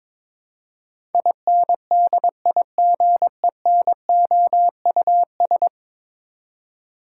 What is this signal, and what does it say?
Morse code, heard as a keyed tone, about 22 wpm, 700 Hz. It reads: INDIGENOUS
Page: morse-code-ninja-practice-22wpm